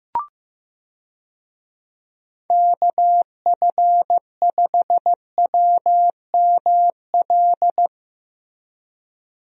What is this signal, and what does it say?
Morse code, heard as a keyed tone, about 15 wpm, 700 Hz. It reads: KF5WML
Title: USA Call Signs - 6 Characters - Repeated 3x - 15wpm